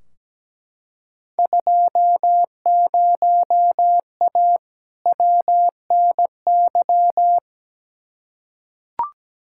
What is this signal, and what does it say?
Morse code, heard as a keyed tone, about 17 wpm, 700 Hz. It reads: 20A WNY
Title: Field Day Exchanges Encore - 17wpm